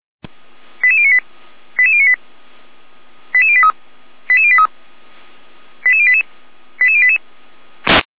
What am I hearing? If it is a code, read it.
Five-tone selective call (ZVEI), heard as quick runs of five tones, tones of about 70 ms. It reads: 88098 88098 88083 88083 88088 88088